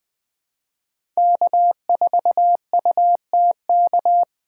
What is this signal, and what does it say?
Morse code, heard as a keyed tone, about 21 wpm, 700 Hz. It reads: K4UTK
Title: USA Call Signs - 5 Characters - 4x Character Spacing - 20wpm